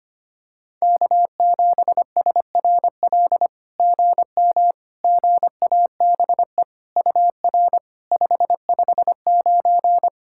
Morse code, heard as a keyed tone, about 25 wpm, 690 Hz. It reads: K7SRL GM GABE UR 559
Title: SOTA QSO Practice 25wpm